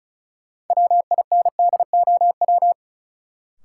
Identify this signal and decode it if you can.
Morse code, heard as a keyed tone, about 36 wpm, 700 Hz. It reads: WINDOW
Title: Sentences from Top 1600 Words - New Word Drill - 35wpm